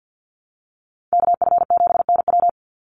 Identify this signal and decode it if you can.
Morse code, heard as a keyed tone, about 50 wpm, 700 Hz. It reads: KF7NW
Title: USA Call Signs 50wpm